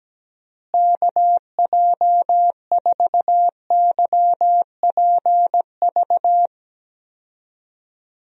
Morse code, heard as a keyed tone, about 17 wpm, 700 Hz. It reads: KJ4YPV
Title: USA Call Signs - 6 Characters - Repeated 3x - 17wpm